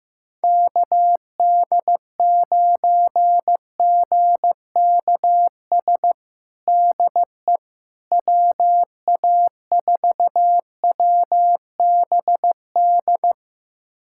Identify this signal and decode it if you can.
Morse code, heard as a keyed tone, about 16 wpm, 700 Hz. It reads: KD9GKS DE WA4WBD
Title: USA Callsign DE Callsign - 6 Characters - 15wpm